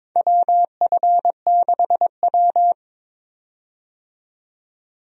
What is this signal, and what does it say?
Morse code, heard as a keyed tone, about 22 wpm, 700 Hz. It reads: WF6W
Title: USA Call Signs - 4 Characters - 22wpm